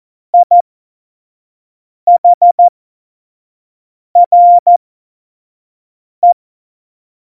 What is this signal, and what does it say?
Morse code, heard as a keyed tone, about 14 wpm, spaced as for 5 wpm, 700 Hz. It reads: IHRE
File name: Morsekurs / Witze / Witz 12. Februar 2019, 25 F70